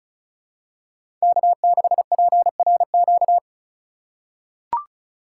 Morse code, heard as a keyed tone, about 36 wpm, 700 Hz. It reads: K6PRQ
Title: USA Call Signs - 5 Characters - 4x Character Spacing - 35wpm